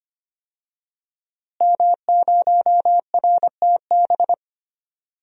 Morse code, heard as a keyed tone, about 25 wpm, 700 Hz. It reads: M0RTB
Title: DXCC-100 Call Signs - 5 Characters - Repeated 2x - 25wpm